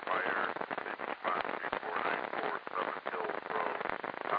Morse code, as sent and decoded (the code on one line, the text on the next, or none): none